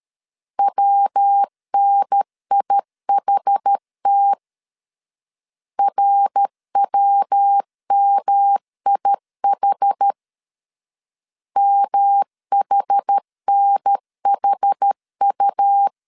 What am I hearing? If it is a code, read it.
Morse code, heard as a keyed tone, about 13 wpm, 800 Hz. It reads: WNIHT RWMIH MHNHU
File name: CW / DET / 1 3 1500C S B 12wpm 2828